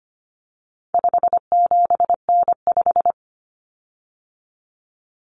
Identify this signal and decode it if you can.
Morse code, heard as a keyed tone, about 25 wpm, 700 Hz. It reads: 57N5
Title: Common Cut Numbers - 4-Digit 25wpm